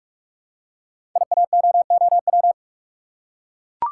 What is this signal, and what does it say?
Morse code, heard as a keyed tone, about 44 wpm, 690 Hz. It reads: IAOOW
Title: Code Groups - 5 Letters - 45wpm